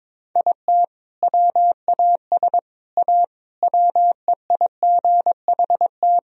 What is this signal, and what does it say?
Morse code, heard as a keyed tone, about 22 wpm, 690 Hz. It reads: IT WAS A WEIGHT
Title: Sentences from Top 1000 Words - Review Part 3 - 22wpm